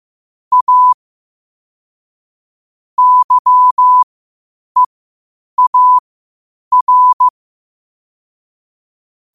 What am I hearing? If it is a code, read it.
Morse code, heard as a keyed tone, about 15 wpm, spaced as for 8 wpm, 1000 Hz. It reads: A YEAR
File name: qotdfiles / 8wpm